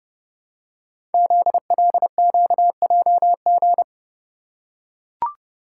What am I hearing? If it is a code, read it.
Morse code, heard as a keyed tone, about 30 wpm, 700 Hz. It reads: ZLQJG
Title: Code Groups - 5 Letters - 30wpm